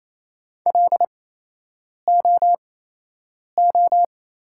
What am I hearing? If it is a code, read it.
Morse code, heard as a keyed tone, about 28 wpm, spaced as for 8 wpm, 700 Hz. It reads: LOO
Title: 8x Character Spacing - Top 100 Words - 28wpm